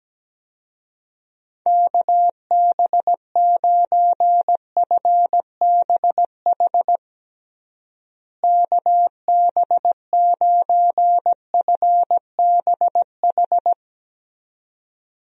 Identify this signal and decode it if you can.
Morse code, heard as a keyed tone, about 17 wpm, 700 Hz. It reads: KB9FBH KB9FBH
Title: USA Call Signs - 6 Characters - Repeated 2x - 17wpm